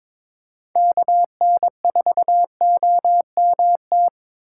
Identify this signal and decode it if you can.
Morse code, heard as a keyed tone, about 22 wpm, 700 Hz. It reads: KN4OMT
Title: USA Call Signs - 6 Characters - 2x Character Spacing - 22wpm